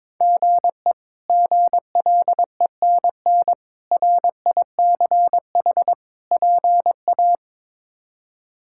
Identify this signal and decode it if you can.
Morse code, heard as a keyed tone, about 22 wpm, 700 Hz. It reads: GE GLENN RICH PA
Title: K1USN SST QSO Practice - 22wpm